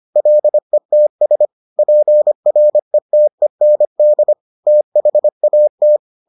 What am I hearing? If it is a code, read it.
Morse code, heard as a keyed tone, about 25 wpm, 600 Hz. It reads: LETS PRETEND THAT